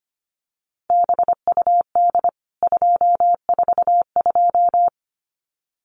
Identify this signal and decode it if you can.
Morse code, heard as a keyed tone, about 25 wpm, 700 Hz. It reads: BUD 242
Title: CWT Exchange 25wpm